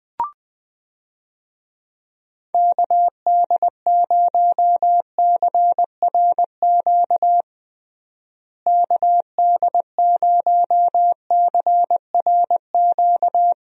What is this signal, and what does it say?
Morse code, heard as a keyed tone, about 20 wpm, 700 Hz. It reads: KD0CRQ KD0CRQ
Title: USA Call Signs - 6 Characters - Repeated 3x - 20wpm